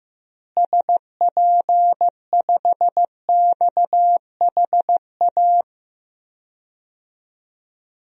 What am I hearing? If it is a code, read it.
Morse code, heard as a keyed tone, about 15 wpm, 700 Hz. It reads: SP5XHA
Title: DXCC-100 Call Signs - 6 Characters - Repeated 2x - 15wpm